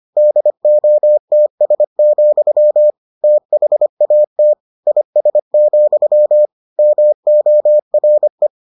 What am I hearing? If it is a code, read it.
Morse code, heard as a keyed tone, about 25 wpm, 600 Hz. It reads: DOTS, THAT IS, MORE